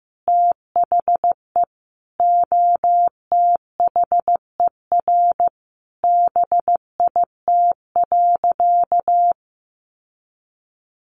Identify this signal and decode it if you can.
Morse code, heard as a keyed tone, about 15 wpm, 700 Hz. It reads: THE OTHER BIT.
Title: Sentences from Top 900 Words 15wpm